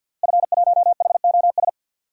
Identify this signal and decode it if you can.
Morse code, heard as a keyed tone, about 51 wpm, 700 Hz. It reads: F1HOS
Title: DXCC-100 Call Signs - 5 Characters - Repeated 2x - 50wpm